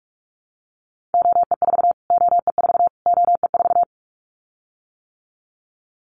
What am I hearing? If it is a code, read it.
Morse code, heard as a keyed tone, about 44 wpm, 700 Hz. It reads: OE4 OE4 OE4
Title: Callsign Prefixes for Top 25 DXCC - Repeated 3x - 45wpm